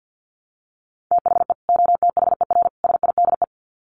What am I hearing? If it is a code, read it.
Morse code, heard as a keyed tone, about 50 wpm, 700 Hz. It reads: THE OTHER SIDE
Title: Sets of 3 Words 50wpm